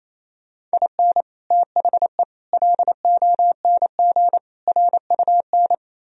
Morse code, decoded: IN THE LONG RUN

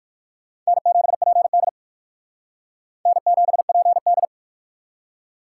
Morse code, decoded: N7PD N7PD

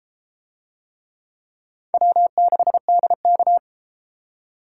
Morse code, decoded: W6DK

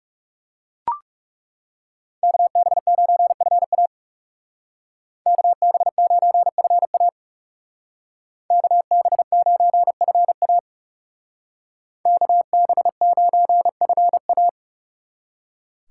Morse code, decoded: KB9FA KB9FA KB9FA KB9FA

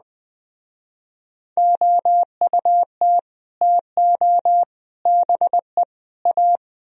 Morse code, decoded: OUT TO BE A

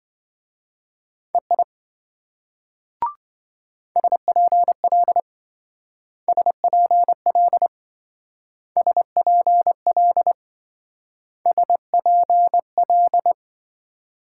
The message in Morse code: EI SPL SPL SPL SPL

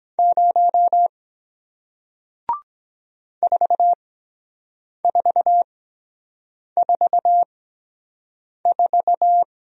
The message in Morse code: 0 4444

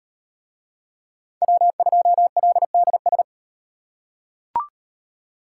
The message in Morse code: W2LDS

